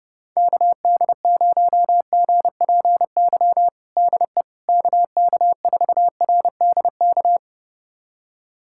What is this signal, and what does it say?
Morse code, heard as a keyed tone, about 30 wpm, 700 Hz. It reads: KD0GPY DE KK4RDK
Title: USA Callsign DE Callsign 30wpm